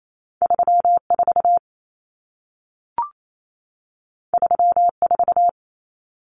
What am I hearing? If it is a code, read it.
Morse code, heard as a keyed tone, about 28 wpm, 700 Hz. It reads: 34 34
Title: Common Cut Numbers - 2-Digit 28wpm